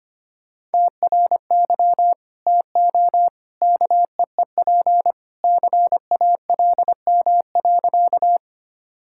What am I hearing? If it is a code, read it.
Morse code, heard as a keyed tone, about 25 wpm, 700 Hz. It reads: TRY TO KEEP CALM.
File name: practice / 25wpm / Sentences with 2-Word Phrases - 25wpm